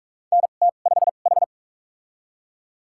Morse code, heard as a keyed tone, about 42 wpm, spaced as for 31 wpm, 700 Hz. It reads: NT5H
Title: USA Call Signs - 2x Character Spacing - 45wpm